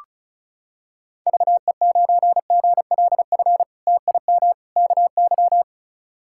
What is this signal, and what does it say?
Morse code, heard as a keyed tone, about 36 wpm, 700 Hz. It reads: VE9GLF TIM KY
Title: K1USN SST QSO Practice - 35wpm